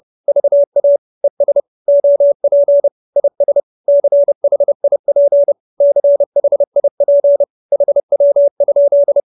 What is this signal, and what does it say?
Morse code, heard as a keyed tone, about 30 wpm, 570 Hz. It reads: VA ES OP IS CHIP CHIP HW?
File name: QSO316 30